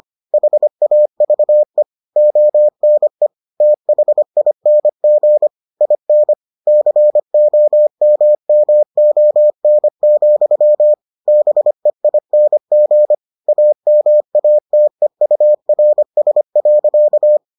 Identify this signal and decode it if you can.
Morse code, heard as a keyed tone, about 25 wpm, 600 Hz. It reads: HAVE ONE THING IN COMMON, BEING AMATEURS.